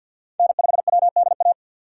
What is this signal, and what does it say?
Morse code, heard as a keyed tone, about 52 wpm, 700 Hz. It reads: N5WDA
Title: USA Call Signs - 5 Characters - 4x Character Spacing - 50wpm